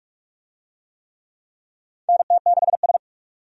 Morse code, heard as a keyed tone, about 41 wpm, 700 Hz. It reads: NT6S
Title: USA Call Signs - 4 Characters - 45wpm